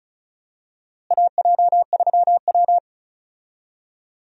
Morse code, heard as a keyed tone, about 36 wpm, 700 Hz. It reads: AJ3W
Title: USA Call Signs - 4 Characters - Repeated 2x - 35wpm